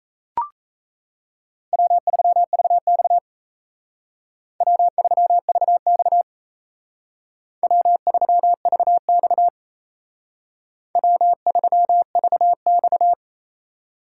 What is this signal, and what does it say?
Morse code, heard as a keyed tone, about 34 wpm, 700 Hz. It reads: W3VX W3VX W3VX W3VX
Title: Speed Racing - USA Call Signs 28wpm